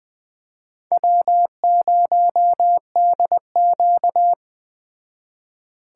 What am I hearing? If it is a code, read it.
Morse code, heard as a keyed tone, about 21 wpm, 700 Hz. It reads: W0DQ
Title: Speed Racing - USA 1x2 Call Signs 17wpm